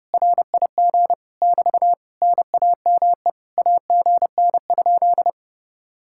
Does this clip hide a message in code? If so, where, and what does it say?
Yes: Morse code from 0.1 to 5.3 s, heard as a keyed tone, about 33 wpm, 700 Hz. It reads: RIG <BT> NAME AGN?